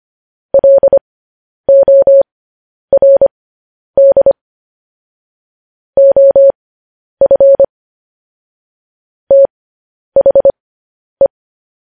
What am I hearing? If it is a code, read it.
Morse code, heard as a keyed tone, about 25 wpm, spaced as for 10 wpm, 550 Hz. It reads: LORD OF THE
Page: news headlines in morse code at 10 wpm